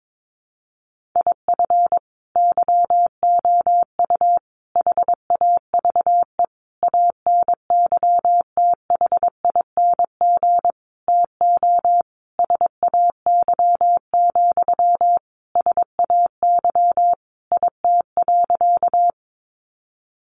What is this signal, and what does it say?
Morse code, heard as a keyed tone, about 22 wpm, 700 Hz. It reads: IF YOU HAVE ANYTHING TO SAY, SAY IT.
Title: Sentences from Top 1000 Words 22wpm